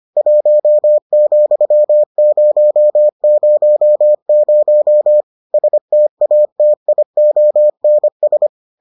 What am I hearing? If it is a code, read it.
Morse code, heard as a keyed tone, about 25 wpm, 600 Hz. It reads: 1,000 STATIONS